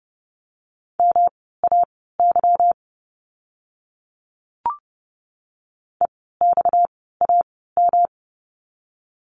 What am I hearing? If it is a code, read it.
Morse code, heard as a keyed tone, about 30 wpm, spaced as for 17 wpm, 700 Hz. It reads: MAY EXAM
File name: course / 30:10wpm / 230 Lesson 37f - Letter x - Words 30wpm